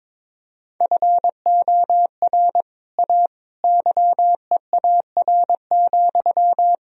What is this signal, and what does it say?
Morse code, heard as a keyed tone, about 22 wpm, 700 Hz. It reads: FOR A YEAR,
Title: Sentences from Top 1000 Words - Review Part 1 - 22wpm